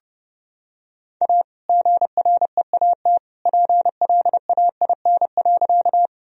Morse code, heard as a keyed tone, about 30 wpm, 700 Hz. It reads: A GREAT PLAIN.
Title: Sentences from Top 1000 Words - Review Part 2 - 30wpm